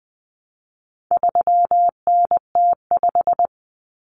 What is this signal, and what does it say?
Morse code, heard as a keyed tone, about 20 wpm, 700 Hz. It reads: 3NT5
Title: Common Cut Numbers - 4-Digit 20wpm